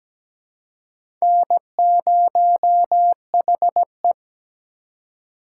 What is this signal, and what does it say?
Morse code, heard as a keyed tone, about 17 wpm, 700 Hz. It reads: N0HE
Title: USA Call Signs - 4 Characters - 6x Character Spacing - 17wpm